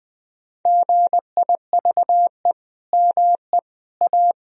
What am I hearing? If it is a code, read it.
Morse code, heard as a keyed tone, about 20 wpm, 700 Hz. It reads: GIVE ME A
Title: Sentences from Top 1000 Words - Review Part 4 - 20wpm